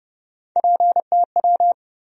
Morse code, heard as a keyed tone, about 29 wpm, 700 Hz. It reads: PTW
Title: USA Call Sign Postfixes - 3 Letters 30wpm